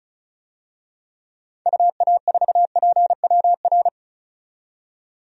Morse code, heard as a keyed tone, about 36 wpm, 700 Hz. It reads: UA4PWR
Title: DXCC-100 Call Signs - 6 Characters - 35wpm